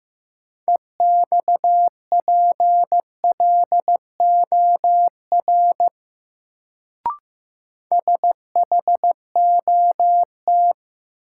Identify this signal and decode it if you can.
Morse code, heard as a keyed tone, about 15 wpm, 700 Hz. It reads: EXPLOR SHOT
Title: Sentences from Top 1100 Words - New Word Drill Alt - 15wpm